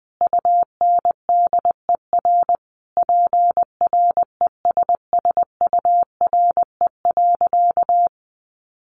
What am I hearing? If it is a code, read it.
Morse code, heard as a keyed tone, about 20 wpm, 700 Hz. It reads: UNDER PRESSURE.
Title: Sentences from Top 1400 Words 20wpm